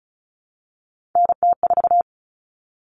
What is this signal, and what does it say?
Morse code, heard as a keyed tone, about 34 wpm, 700 Hz. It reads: NT4
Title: All Cut Numbers - 3-Digit 35wpm